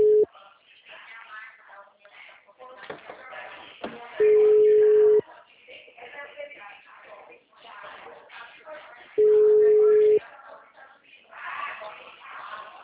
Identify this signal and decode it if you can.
Ringback tone (Europe, 425 Hz), with speech.
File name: out- -2010- - - .6143